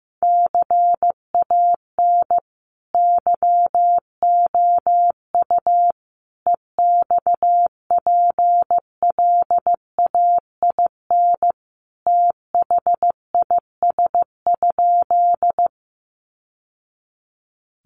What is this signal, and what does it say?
Morse code, heard as a keyed tone, about 15 wpm, 700 Hz. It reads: CAN YOU EXPLAIN THIS?
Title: Sentences from Top 1100 Words 15wpm